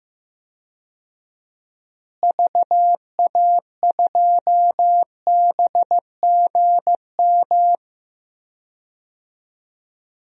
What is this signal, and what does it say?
Morse code, heard as a keyed tone, about 15 wpm, 700 Hz. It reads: VA2BGM
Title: DXCC-100 Call Signs - 6 Characters - Repeated 2x - 15wpm